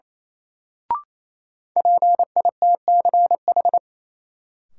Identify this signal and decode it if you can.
Morse code, heard as a keyed tone, about 27 wpm, 700 Hz. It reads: PITCH
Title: Sentences from Top 1300 Words - New Word Drill - 28wpm